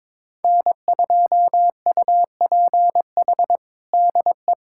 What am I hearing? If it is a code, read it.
Morse code, heard as a keyed tone, about 21 wpm, 700 Hz. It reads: N2UPH DE
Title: USA Callsign DE Callsign - 5 Characters - 22wpm